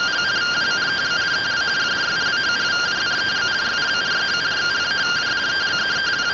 Two-tone teleprinter signal, about 170 Hz shift.
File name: findings / 100 surview 0.gen-23 rtty rx